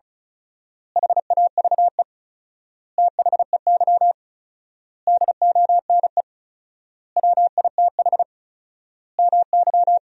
Morse code, heard as a keyed tone, about 36 wpm, 700 Hz. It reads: HAVE THEY DONE WITH MY
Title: 4x Word-Spacing - Sentences from Top 500 Words - 35wpm